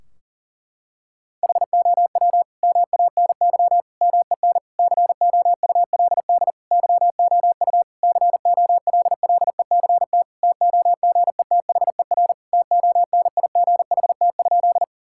Morse code, heard as a keyed tone, about 37 wpm, 700 Hz. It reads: HOW MANY MEN COULD YOU COLLECT TOGETHER TONIGHT?
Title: Sentences from Top 1600 Words 40wpm